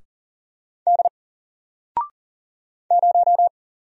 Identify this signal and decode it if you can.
Morse code, heard as a keyed tone, about 40 wpm, 700 Hz. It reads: D 0
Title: Single Letter-Number Alt 40wpm